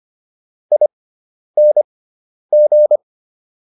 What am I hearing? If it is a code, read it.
Morse code, heard as a keyed tone, about 25 wpm, spaced as for 10 wpm, 600 Hz. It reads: ING